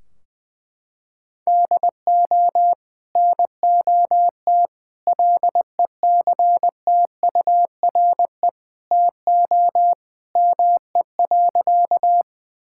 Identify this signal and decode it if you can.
Morse code, heard as a keyed tone, about 20 wpm, 700 Hz. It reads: DO NOT LECTURE TO ME.
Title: Sentences from Top 1600 Words 20wpm